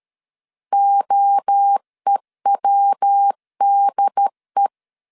Morse code, heard as a keyed tone, about 13 wpm, 800 Hz. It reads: OEWDE
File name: CW / DET / 1 3 1500C S C 12wpm 2750